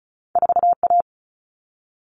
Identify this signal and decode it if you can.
Morse code, heard as a keyed tone, about 36 wpm, 700 Hz. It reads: 4A
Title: All Cut Numbers - 2-Digit 35wpm